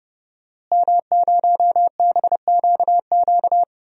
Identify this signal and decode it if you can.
Morse code, heard as a keyed tone, about 31 wpm, 700 Hz. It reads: M0BQQ
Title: DXCC-100 Call Signs - 5 Characters - Repeated 2x - 30wpm